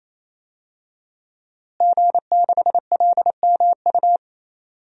Morse code, heard as a keyed tone, about 28 wpm, 700 Hz. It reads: G6LMU
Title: DXCC-100 Call Signs - 5 Characters - Repeated 3x - 28wpm